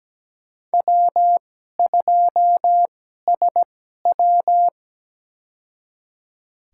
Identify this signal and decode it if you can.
Morse code, heard as a keyed tone, about 17 wpm, spaced as for 12 wpm, 700 Hz. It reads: W2SW
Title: USA Call Signs - 4 Characters - 2x Character Spacing - 17wpm